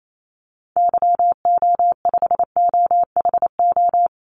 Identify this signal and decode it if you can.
Morse code, heard as a keyed tone, about 28 wpm, 700 Hz. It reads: YO5OHO
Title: Call Signs Repeated 3x - 28wpm